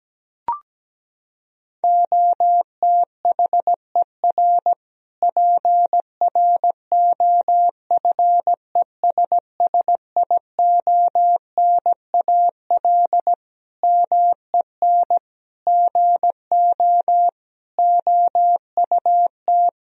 Morse code, heard as a keyed tone, about 17 wpm, 700 Hz. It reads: OTHER PROFESSIONAL MEN GO OUT
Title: Sentences from Top 1000 Words - Review Part 4 - 17wpm